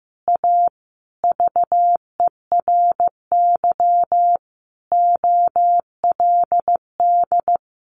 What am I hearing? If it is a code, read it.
Morse code, heard as a keyed tone, about 15 wpm, 700 Hz. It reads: A VERY OLD
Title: A Sound Of Thunder 15wpm